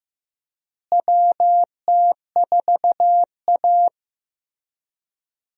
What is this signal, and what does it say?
Morse code, heard as a keyed tone, about 15 wpm, 700 Hz. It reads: WT4A
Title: USA Call Signs - 4 Characters - Repeated 3x - 15wpm